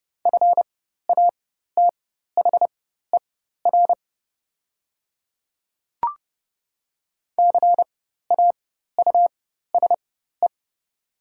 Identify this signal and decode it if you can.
Morse code, heard as a keyed tone, about 30 wpm, spaced as for 14 wpm, 700 Hz. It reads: FATHER CAUSE
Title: 4x Character Spacing - Top 300 Words - 30wpm